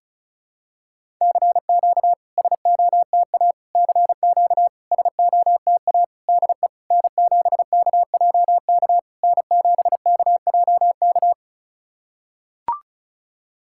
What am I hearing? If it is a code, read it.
Morse code, heard as a keyed tone, about 36 wpm, 690 Hz. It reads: CQ SOTA CQ SOTA DE N7KJK N7KJK
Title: SOTA QSO Practice 35wpm